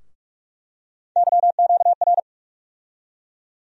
Morse code, heard as a keyed tone, about 41 wpm, 700 Hz. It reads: YQR
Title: USA Call Sign Postfixes - 3 Letters 45wpm